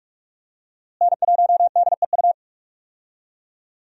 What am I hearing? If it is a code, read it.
Morse code, heard as a keyed tone, about 44 wpm, 700 Hz. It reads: N1DEU
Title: USA Call Signs - 5 Characters - Repeated 2x - 45wpm